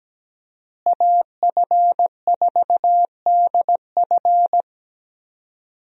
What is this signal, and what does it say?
Morse code, heard as a keyed tone, about 17 wpm, 700 Hz. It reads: AF4DF
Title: USA Call Signs - 5 Characters - Repeated 2x - 17wpm